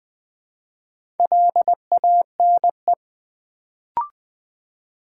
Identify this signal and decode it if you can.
Morse code, heard as a keyed tone, about 20 wpm, 700 Hz. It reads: LANE
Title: Build A Word - 2 Letters At A Time - 4-Letter Words - 20wpm